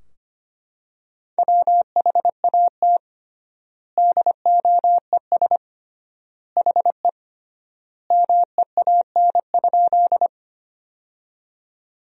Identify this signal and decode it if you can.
Morse code, heard as a keyed tone, about 25 wpm, 700 Hz. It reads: WHAT DOES HE MEAN?
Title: 3x Word-Spacing - Sentences from Top 200 Words - 25wpm